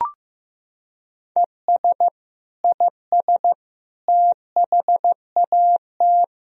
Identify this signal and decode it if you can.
Morse code, heard as a keyed tone, about 15 wpm, 700 Hz. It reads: ES IS THAT
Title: Sentences from Top 100 Words and Abbreviations 15wpm